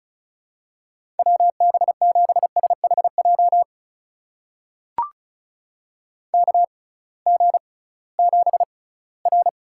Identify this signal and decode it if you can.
Morse code, heard as a keyed tone, about 36 wpm, spaced as for 20 wpm, 700 Hz. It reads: WB7SHJ KG7R
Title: USA Call Signs - 6 Characters - 6x Character Spacing - 35wpm